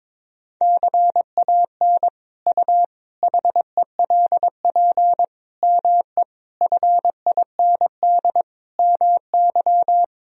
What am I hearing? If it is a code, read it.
Morse code, heard as a keyed tone, about 21 wpm, 700 Hz. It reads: CAN U HELP ME FIND MY
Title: Sentences from Top 200 Words and Abbreviations 22wpm